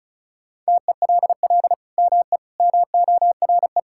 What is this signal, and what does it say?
Morse code, heard as a keyed tone, about 36 wpm, 700 Hz. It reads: TELL ME MORE
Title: Sentences from Top 1000 Words - Review Part 5 - 35wpm